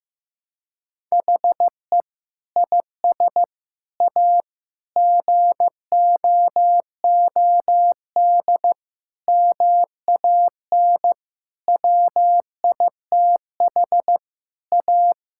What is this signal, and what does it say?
Morse code, heard as a keyed tone, about 15 wpm, 700 Hz. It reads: HE IS A GOOD MAN WITH A
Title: Sentences from Top 1000 Words - Review Part 6 - 15wpm